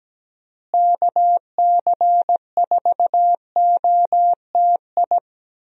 Morse code, heard as a keyed tone, about 17 wpm, 700 Hz. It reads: KC4OTI
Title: USA Call Signs - 6 Characters - Repeated 2x - 17wpm